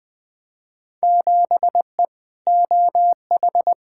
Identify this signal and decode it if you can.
Morse code, heard as a keyed tone, about 21 wpm, 700 Hz. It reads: 7E OH
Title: Field Day Exchanges Encore - 20wpm